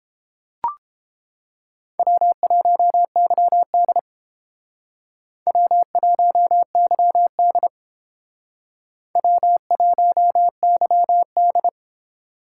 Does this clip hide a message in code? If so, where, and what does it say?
Yes: Morse code from 2.0 to 11.7 s, heard as a keyed tone, about 28 wpm, 700 Hz. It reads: W1YD W1YD W1YD